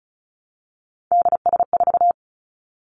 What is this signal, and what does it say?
Morse code, heard as a keyed tone, about 36 wpm, 700 Hz. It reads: DS4
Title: Three Challenging Letter-Number Combinations 35wpm